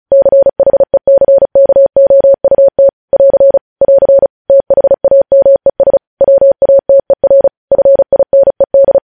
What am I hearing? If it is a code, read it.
Morse code, heard as a keyed tone, about 36 wpm, 550 Hz. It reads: CHECKOUT <AR> <AR> THAMES WATER FINED
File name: podcast / News headlines-1x-35x35